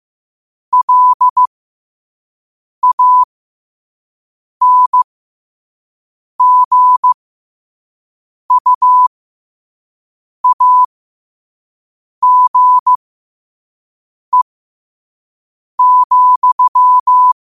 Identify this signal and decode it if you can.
Morse code, heard as a keyed tone, about 15 wpm, spaced as for 5 wpm, 1000 Hz. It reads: LANGUAGE,
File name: qotdfiles / 5wpm